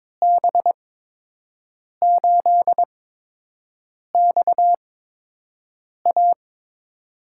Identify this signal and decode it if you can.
Morse code, heard as a keyed tone, about 22 wpm, spaced as for 6 wpm, 700 Hz. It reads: B8XA
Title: USA Call Signs - 8x Character Spacing - 22wpm